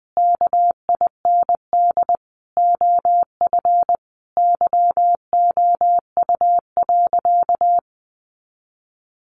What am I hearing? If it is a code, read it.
Morse code, heard as a keyed tone, about 20 wpm, 700 Hz. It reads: KIND OF YOU.
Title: Sentences from Top 200 Words 20wpm